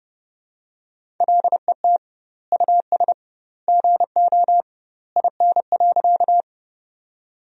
Morse code, heard as a keyed tone, about 30 wpm, 700 Hz. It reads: LET US GO IN.
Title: 2x Word-Spacing - Sentences from Top 300 Words - 30wpm